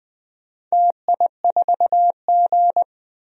Morse code, heard as a keyed tone, about 20 wpm, 700 Hz. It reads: TI4G
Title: DXCC-100 Call Signs - 4 Characters - 20wpm